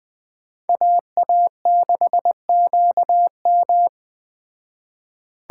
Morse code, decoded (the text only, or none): AA6QM